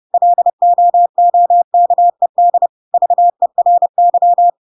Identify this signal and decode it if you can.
Morse code, heard as a keyed tone, about 30 wpm, 700 Hz. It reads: LOOKED VERY
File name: Pinocchio / Pinocchio ch5 30 30 ik1hge